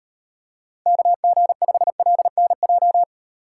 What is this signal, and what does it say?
Morse code, heard as a keyed tone, about 38 wpm, 700 Hz. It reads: KG5LNJ